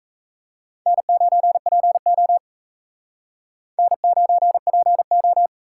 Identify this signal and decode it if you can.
Morse code, heard as a keyed tone, about 35 wpm, 700 Hz. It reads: N9PO N9PO
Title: Speed Racing - USA 1x2 Call Signs 28wpm